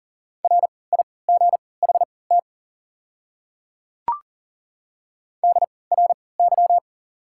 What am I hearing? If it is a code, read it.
Morse code, heard as a keyed tone, about 40 wpm, spaced as for 23 wpm, 690 Hz. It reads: RIGHT DRY
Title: 3x Character Spacing - Top 500 Words - 40wpm